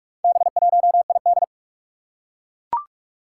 Morse code, heard as a keyed tone, about 42 wpm, 700 Hz. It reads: B1ID
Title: Four Challenging Letter-Number Combinations 45wpm